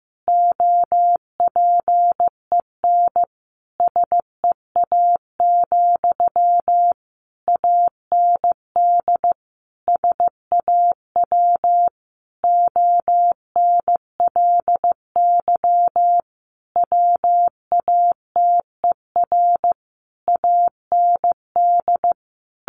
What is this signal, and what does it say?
Morse code, heard as a keyed tone, about 15 wpm, 700 Hz. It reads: OPEN SEA, AND SAW ONLY WATER AND
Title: Sentences from Top 1200 Words 15wpm